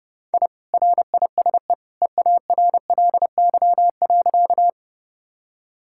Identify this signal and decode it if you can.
Morse code, heard as a keyed tone, about 30 wpm, 700 Hz. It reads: I RISE EARLY.
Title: Sentences from Top 1000 Words - Review Part 6 - 30wpm